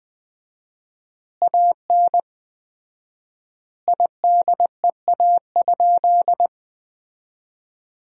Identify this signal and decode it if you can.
Morse code, heard as a keyed tone, about 20 wpm, 700 Hz. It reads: AN IDEA?
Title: 4x Word-Spacing - Sentences from Top 400 Words - 20wpm